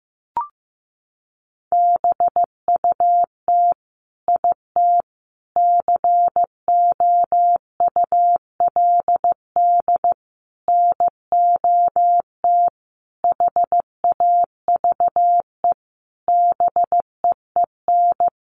Morse code, heard as a keyed tone, about 15 wpm, 700 Hz. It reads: BUT IT COULD NOT HAVE BEEN
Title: Sentences from Top 1400 Words 15wpm